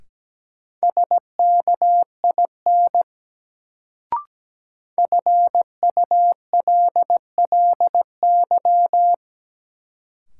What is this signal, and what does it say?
Morse code, heard as a keyed tone, about 17 wpm, 700 Hz. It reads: SKIN FULLY
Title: Sentences from Top 1300 Words - New Word Drill Alt - 17wpm